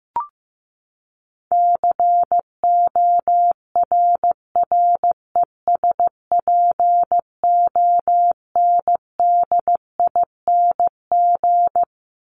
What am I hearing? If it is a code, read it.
Morse code, heard as a keyed tone, about 15 wpm, 700 Hz. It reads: CORRESPONDING
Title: ING Suffix - Top 100 Words with Base Words 15wpm